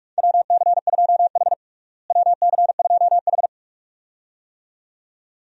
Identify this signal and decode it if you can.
Morse code, heard as a keyed tone, about 41 wpm, 700 Hz. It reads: WC2H WC2H
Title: USA Call Signs - 4 Characters - Repeated 2x - 45wpm